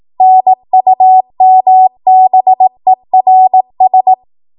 Morse code, heard as a keyed tone, about 18 wpm, 750 Hz. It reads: NUMBERS
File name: Archive / 18 WPM / 18WPM